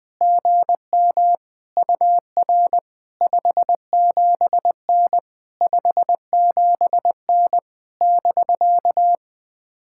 Morse code, heard as a keyed tone, about 21 wpm, 700 Hz. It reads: GM UR 57N 57N <BK>
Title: POTA QSO Practice - 20wpm